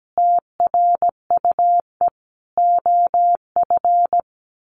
Morse code, heard as a keyed tone, about 17 wpm, 700 Hz. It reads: TRUE OF
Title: Sentences from Top 1400 Words 17wpm